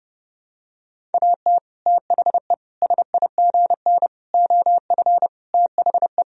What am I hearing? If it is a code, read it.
Morse code, heard as a keyed tone, about 30 wpm, 700 Hz. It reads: AT THE SIGN OF THE
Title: Sentences from Top 1000 Words - Review Part 1 - 30wpm